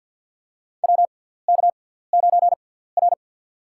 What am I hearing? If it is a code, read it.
Morse code, heard as a keyed tone, about 53 wpm, spaced as for 17 wpm, 690 Hz. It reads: WK9R
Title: USA Call Signs - 4 Characters - 6x Character Spacing - 50wpm